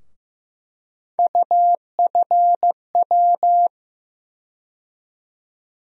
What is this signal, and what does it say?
Morse code, heard as a keyed tone, about 15 wpm, 700 Hz. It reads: UFW
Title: USA Call Sign Postfixes - 3 Letters 15wpm